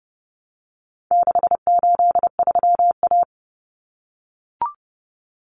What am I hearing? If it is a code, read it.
Morse code, heard as a keyed tone, about 28 wpm, 700 Hz. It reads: 683A
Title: Common Cut Numbers - 4-Digit 30wpm